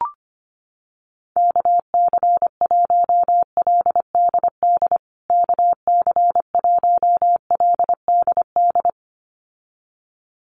Morse code, heard as a keyed tone, about 25 wpm, 700 Hz. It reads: KC1LDD KC1LDD
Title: USA Call Signs Repeated 2x 25wpm